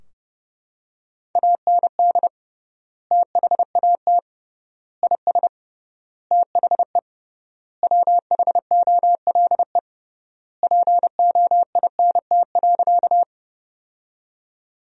Morse code, heard as a keyed tone, about 30 wpm, 700 Hz. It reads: AND THAT IS THE WHOLE POINT.
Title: 3x Word-Spacing - Sentences from Top 400 Words - 30wpm